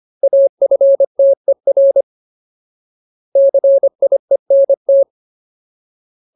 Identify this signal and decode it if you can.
Morse code, heard as a keyed tone, about 25 wpm, 550 Hz. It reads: AFTER CIENT
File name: Ultimate Word List / 25WPM Ultimate